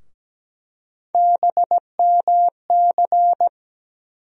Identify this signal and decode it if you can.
Morse code, heard as a keyed tone, about 17 wpm, 700 Hz. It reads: BMC
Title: USA Call Sign Postfixes - 3 Letters 17wpm